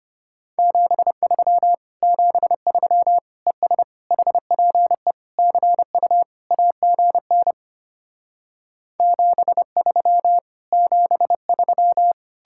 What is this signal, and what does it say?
Morse code, heard as a keyed tone, about 33 wpm, 700 Hz. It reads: 73 73 ES HPE CU AGN 73 73